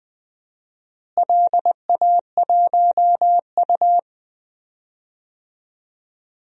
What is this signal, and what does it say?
Morse code, heard as a keyed tone, about 20 wpm, 700 Hz. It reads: LA1U
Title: DXCC-100 Call Signs - 4 Characters - Repeated 3x - 20wpm